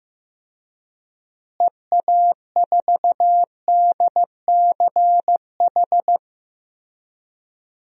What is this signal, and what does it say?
Morse code, heard as a keyed tone, about 16 wpm, 700 Hz. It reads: EA4DCH
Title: DXCC-100 Call Signs - 6 Characters - Repeated 2x - 15wpm